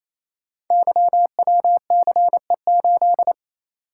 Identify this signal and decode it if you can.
Morse code, heard as a keyed tone, about 29 wpm, 700 Hz. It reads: YWCE8
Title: Code Groups - 5 Letter:Numbers - 28wpm